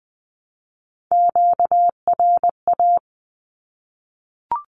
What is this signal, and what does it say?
Morse code, heard as a keyed tone, about 20 wpm, 700 Hz. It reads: QRA
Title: Q-Codes 20wpm